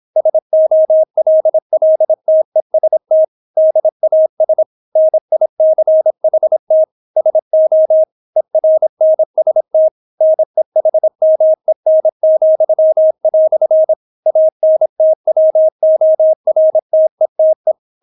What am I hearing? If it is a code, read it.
Morse code, heard as a keyed tone, about 26 wpm, 630 Hz. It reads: SOLLTEST DAS NICHT SO ERNST NEHMEN," ANTWORTETE